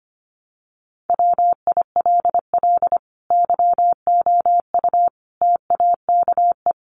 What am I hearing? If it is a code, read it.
Morse code, heard as a keyed tone, about 25 wpm, 700 Hz. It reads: WILL YOU TAKE
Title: Sentences from Top 900 Words 25wpm